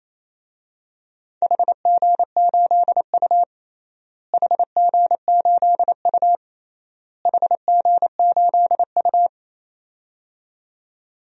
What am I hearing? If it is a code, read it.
Morse code, heard as a keyed tone, about 27 wpm, 700 Hz. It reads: HG8U HG8U HG8U
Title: DXCC-100 Call Signs - 4 Characters - Repeated 3x - 28wpm